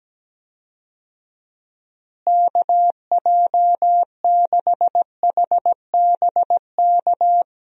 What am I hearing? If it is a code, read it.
Morse code, heard as a keyed tone, about 17 wpm, 700 Hz. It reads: KJ6HBK
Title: USA Call Signs - 6 Characters - Repeated 3x - 17wpm